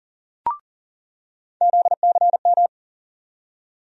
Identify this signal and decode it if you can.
Morse code, heard as a keyed tone, about 40 wpm, 700 Hz. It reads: ZCM
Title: USA Call Sign Postfixes - 3 Letters 40wpm